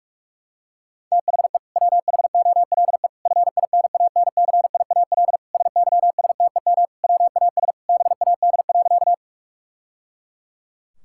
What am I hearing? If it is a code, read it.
Morse code, heard as a keyed tone, about 41 wpm, 700 Hz. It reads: THE WHOLE FINANCIAL SYSTEM WAS BAD.
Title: Sentences from Top 1000 Words - Review Part 5 - 45wpm